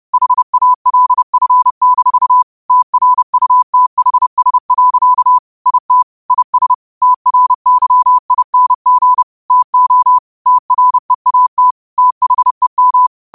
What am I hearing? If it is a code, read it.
Morse code, heard as a keyed tone, about 30 wpm, 1000 Hz. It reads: HALF-TRUTHS. IT IS TRYING TO TREAT THEM
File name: qotdfiles / 30wpm